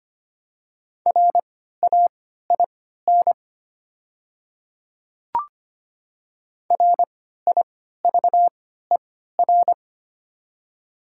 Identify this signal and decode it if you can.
Morse code, heard as a keyed tone, about 25 wpm, spaced as for 14 wpm, 700 Hz. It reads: RAIN RIVER
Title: 3x Character Spacing - Top 300 Words - 25wpm